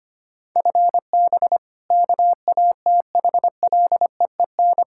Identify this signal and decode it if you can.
Morse code, heard as a keyed tone, about 25 wpm, 700 Hz. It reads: FB KATHLEEN
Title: Practice QSO - Exchange 3 - More Info - 25wpm